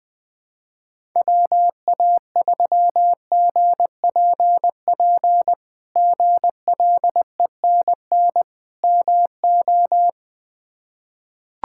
Morse code, heard as a keyed tone, about 20 wpm, 700 Hz. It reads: WA3GPP GLENN MO
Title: K1USN SST QSO Practice - 20wpm